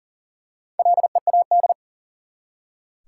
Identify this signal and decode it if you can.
Morse code, heard as a keyed tone, about 40 wpm, 700 Hz. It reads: LEAD
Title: Four Letter Words - 40wpm